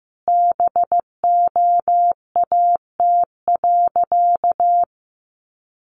Morse code, heard as a keyed tone, about 15 wpm, 700 Hz. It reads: BOAT.
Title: Sentences from Top 1500 Words 15wpm